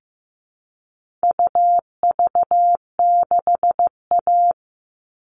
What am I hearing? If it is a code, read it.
Morse code, heard as a keyed tone, about 15 wpm, 700 Hz. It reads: UV6A
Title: All Cut Numbers - 4-Digit 15wpm